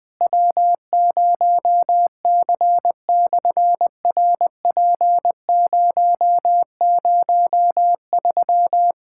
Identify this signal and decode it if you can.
Morse code, heard as a keyed tone, about 18 wpm, 700 Hz. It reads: W0C/RP003
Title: SOTA QSO Practice 20wpm